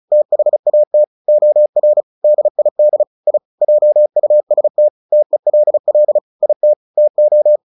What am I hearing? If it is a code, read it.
Morse code, heard as a keyed tone, about 34 wpm, 600 Hz. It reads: THAT OR DID I JUST TELL IT TO
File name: the old man and the sea / the old man and the sea 1 35wpm